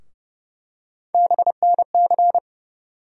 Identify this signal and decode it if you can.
Morse code, heard as a keyed tone, about 30 wpm, 700 Hz. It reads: BNC
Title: USA Call Sign Postfixes - 3 Letters 30wpm